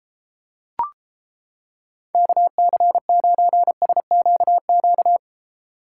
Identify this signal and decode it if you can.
Morse code, heard as a keyed tone, about 33 wpm, 700 Hz. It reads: KC9SQQ